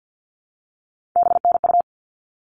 Morse code, heard as a keyed tone, about 50 wpm, 700 Hz. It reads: BNU
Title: All Cut Numbers - 3-Digit 50wpm